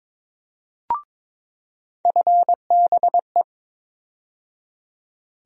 Morse code, heard as a keyed tone, about 22 wpm, 700 Hz. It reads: FBE
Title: USA Call Sign Postfixes - 3 Letters 22wpm